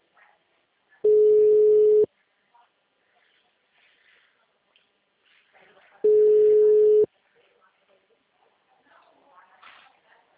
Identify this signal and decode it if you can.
Ringback tone (Europe, 425 Hz), with speech.
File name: out- -2003- - - .23408